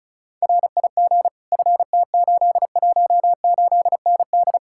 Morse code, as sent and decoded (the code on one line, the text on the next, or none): .-. .. --. / ..-. - ---.. .---- ---.. -. -..
RIG FT818ND